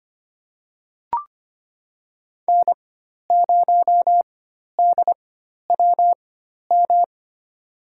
Morse code, heard as a keyed tone, about 25 wpm, spaced as for 12 wpm, 700 Hz. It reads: N0DWM
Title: USA Call Signs - 5 Characters - 4x Character Spacing - 25wpm